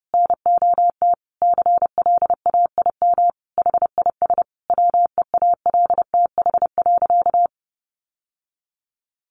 Morse code, heard as a keyed tone, about 30 wpm, 700 Hz. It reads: NOT CLAIM HIS WEALTH.
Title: Sentences from Top 1500 Words 30wpm